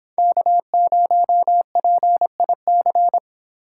Morse code, heard as a keyed tone, about 24 wpm, 700 Hz. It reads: K0PIC